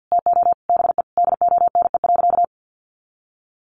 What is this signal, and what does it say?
Morse code, heard as a keyed tone, about 50 wpm, 700 Hz. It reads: TO BE DONE.
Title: Sentences from Top 800 Words 50wpm